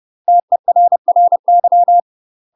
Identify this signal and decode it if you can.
Morse code, heard as a keyed tone, about 30 wpm, 700 Hz. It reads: TERRY
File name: UK MASTER QSOs TRAINING / 30WPM / UK MASTER QSOs 0036